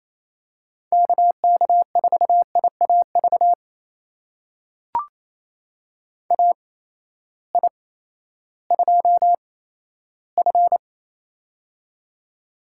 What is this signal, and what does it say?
Morse code, heard as a keyed tone, about 28 wpm, spaced as for 14 wpm, 700 Hz. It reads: KK4IAV AI2F
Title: USA Call Signs - 8x Character Spacing - 28wpm